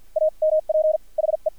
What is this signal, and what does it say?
Morse code, heard as a keyed tone, about 53 wpm, spaced as for 42 wpm, 620 Hz. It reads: AMP HE
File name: archived / CRSnet